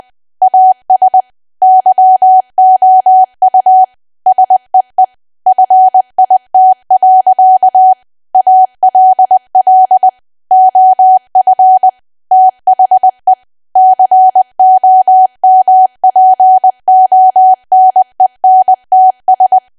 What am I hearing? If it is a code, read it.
Morse code, heard as a keyed tone, about 20 wpm, 750 Hz. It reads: AS YOU SEE FIT. ALL OF THE COMPONENTS
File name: cwa / W1AW-Level3 / 20WPM